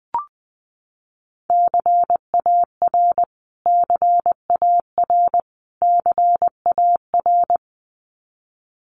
Morse code, heard as a keyed tone, about 20 wpm, 700 Hz. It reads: CAR CAR CAR
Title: Top 500 Words Repeated 20wpm